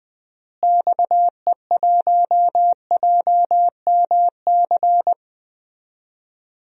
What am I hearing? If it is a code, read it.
Morse code, heard as a keyed tone, about 20 wpm, 700 Hz. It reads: XE1JMC
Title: DXCC-100 Call Signs - 6 Characters - Repeated 2x - 20wpm